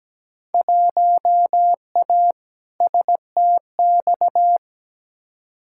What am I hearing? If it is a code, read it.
Morse code, heard as a keyed tone, about 17 wpm, 700 Hz. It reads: 1A STX
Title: Field Day Exchanges Encore - 17wpm